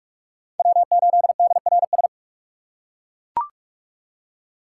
Morse code, heard as a keyed tone, about 41 wpm, 700 Hz. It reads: W8DRS
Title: USA Call Signs - 5 Characters - 3x Character Spacing - 45wpm